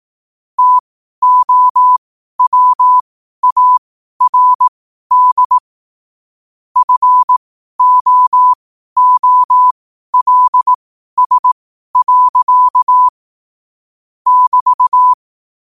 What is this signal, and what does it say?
Morse code, heard as a keyed tone, about 18 wpm, spaced as for 12 wpm, 1000 Hz. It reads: TOWARD FOOLS. <BT>
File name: qotdfiles / 12wpm